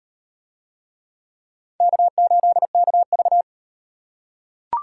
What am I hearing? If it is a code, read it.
Morse code, heard as a keyed tone, about 38 wpm, 700 Hz. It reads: K8KV